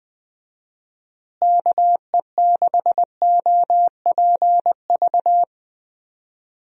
Morse code, heard as a keyed tone, about 20 wpm, 700 Hz. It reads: KE6OPV
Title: USA Call Signs - 6 Characters - Repeated 2x - 20wpm